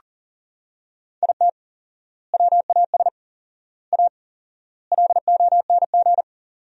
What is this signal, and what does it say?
Morse code, heard as a keyed tone, about 37 wpm, 700 Hz. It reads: IT WAS A LONG
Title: 4x Word-Spacing - Sentences from Top 500 Words - 40wpm